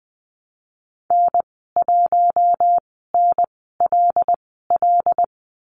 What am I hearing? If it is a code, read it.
Morse code, heard as a keyed tone, about 20 wpm, spaced as for 15 wpm, 700 Hz. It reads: N1NLL
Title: 088 Lesson 15g - Number 5 - Call Signs 20wpm